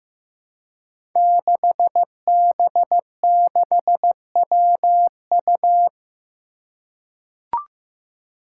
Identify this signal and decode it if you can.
Morse code, heard as a keyed tone, about 16 wpm, 700 Hz. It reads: 6B6WU
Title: Five Challenging Letter-Number Combinations 15wpm